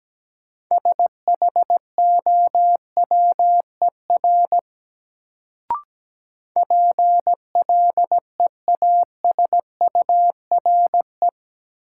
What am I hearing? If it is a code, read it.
Morse code, heard as a keyed tone, about 17 wpm, 700 Hz. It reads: SHOWER PLEASURE
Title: Sentences from Top 1600 Words - New Word Drill - 17wpm